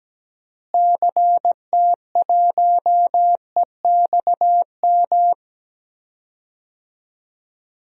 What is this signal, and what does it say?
Morse code, heard as a keyed tone, about 17 wpm, 700 Hz. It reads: CT1EXM
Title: DXCC-100 Call Signs - 6 Characters - Repeated 2x - 17wpm